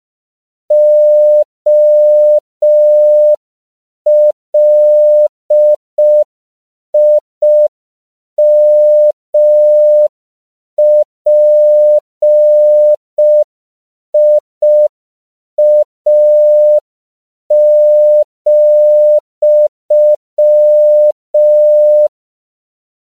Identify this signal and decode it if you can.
Morse code, heard as a keyed tone, about 5 wpm, 600 Hz. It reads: OLIMPIA,
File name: CW / 5PPM